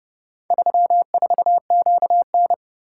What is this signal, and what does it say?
Morse code, heard as a keyed tone, about 29 wpm, 700 Hz. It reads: 34QN